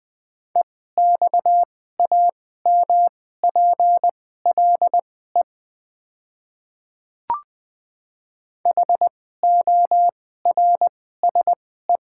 Morse code, heard as a keyed tone, about 20 wpm, spaced as for 14 wpm, 700 Hz. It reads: EXAMPLE HORSE
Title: 2x Character Spacing - Top 500 Words - 20wpm